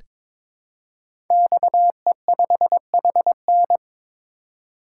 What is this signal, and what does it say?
Morse code, heard as a keyed tone, about 22 wpm, 700 Hz. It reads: XE5HN
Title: Code Groups - 5 Letter:Numbers - 22wpm